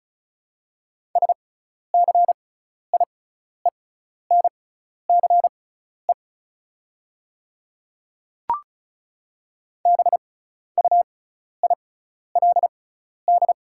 Morse code, heard as a keyed tone, about 33 wpm, spaced as for 12 wpm, 690 Hz. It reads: SCIENCE BUILD
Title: 6x Character Spacing - Top 500 Words - 35wpm